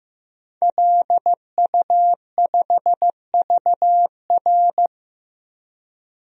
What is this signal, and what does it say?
Morse code, heard as a keyed tone, about 15 wpm, 700 Hz. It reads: LU5VR
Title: DXCC-100 Call Signs - 5 Characters - Repeated 3x - 15wpm